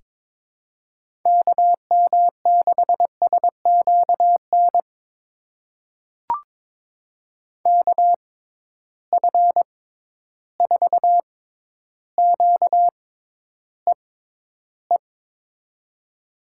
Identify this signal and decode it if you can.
Morse code, heard as a keyed tone, about 22 wpm, spaced as for 11 wpm, 700 Hz. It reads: KM6SQN KF4QEE
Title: USA Call Signs - 6x Character Spacing - 22wpm